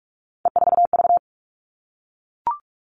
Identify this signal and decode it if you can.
Morse code, heard as a keyed tone, about 44 wpm, 700 Hz. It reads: E4V
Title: All Cut Numbers - 3-Digit 45wpm